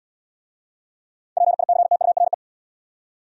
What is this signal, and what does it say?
Morse code, heard as a keyed tone, about 64 wpm, 700 Hz. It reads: PICTURE